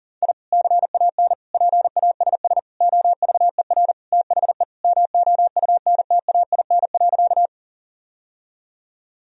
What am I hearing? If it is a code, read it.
Morse code, heard as a keyed tone, about 40 wpm, 700 Hz. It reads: I CAN PASS OVER THE MOUNTAIN.
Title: Sentences from Top 1000 Words - Review Part 1 - 40wpm